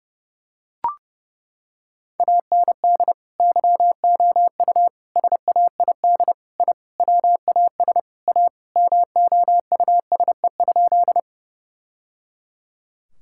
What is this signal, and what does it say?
Morse code, heard as a keyed tone, about 30 wpm, 700 Hz. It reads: AND YOU SAID I WAS A MOUSE?
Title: Sentences from Top 1000 Words - Review Part 5 - 30wpm